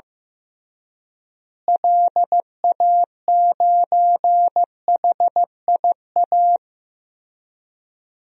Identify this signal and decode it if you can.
Morse code, heard as a keyed tone, about 16 wpm, 700 Hz. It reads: LA9HIA
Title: DXCC-100 Call Signs - 6 Characters - Repeated 3x - 15wpm